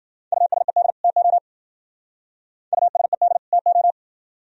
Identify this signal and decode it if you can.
Morse code, heard as a keyed tone, about 52 wpm, 700 Hz. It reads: USED TO USED TO